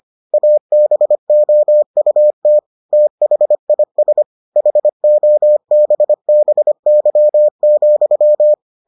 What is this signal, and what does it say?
Morse code, heard as a keyed tone, about 25 wpm, 600 Hz. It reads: ABOUT THIS HOBBY,